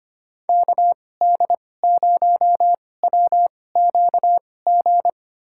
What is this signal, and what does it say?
Morse code, heard as a keyed tone, about 25 wpm, spaced as for 18 wpm, 700 Hz. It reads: KD0WQG
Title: USA Call Signs - 6 Characters - 2x Character Spacing - 25wpm